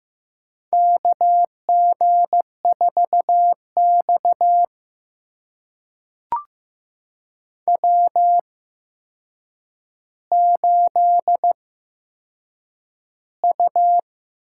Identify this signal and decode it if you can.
Morse code, heard as a keyed tone, about 15 wpm, spaced as for 7 wpm, 700 Hz. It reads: KG4X W8U
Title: USA Call Signs - 4 Characters - 8x Character Spacing - 15wpm